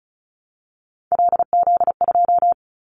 Morse code, heard as a keyed tone, about 34 wpm, 700 Hz. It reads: LZ2
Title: Callsign Prefixes for Top 25 DXCC - 35wpm